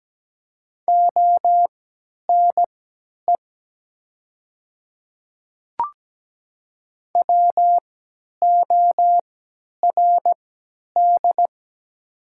Morse code, heard as a keyed tone, about 17 wpm, spaced as for 10 wpm, 700 Hz. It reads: ONE WORD
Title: 3x Character Spacing - Top 100 Words - 17wpm